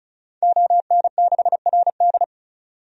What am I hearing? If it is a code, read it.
Morse code, heard as a keyed tone, about 36 wpm, 700 Hz. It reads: ON6RD
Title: DXCC-100 Call Signs - 5 Characters - 35wpm